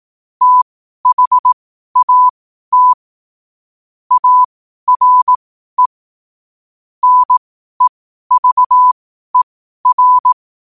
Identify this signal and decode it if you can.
Morse code, heard as a keyed tone, about 18 wpm, spaced as for 12 wpm, 1000 Hz. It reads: THAT ARE NEVER
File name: qotdfiles / 12wpm